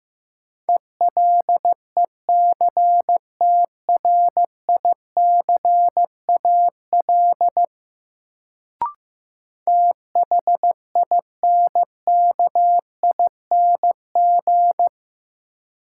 Morse code, heard as a keyed tone, about 16 wpm, 700 Hz. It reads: ELECTRICAL THINKING
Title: Sentences from Top 1000 Words - Review Part 6 - New Word Drill - 15wpm